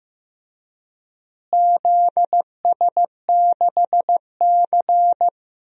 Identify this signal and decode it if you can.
Morse code, heard as a keyed tone, about 15 wpm, 700 Hz. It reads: ZS6C
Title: DXCC-100 Call Signs - 4 Characters - Repeated 3x - 15wpm